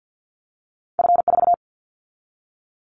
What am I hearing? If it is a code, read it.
Morse code, heard as a keyed tone, about 50 wpm, 700 Hz. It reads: F4
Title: Callsign Prefixes for Top 50 DXCC - 50wpm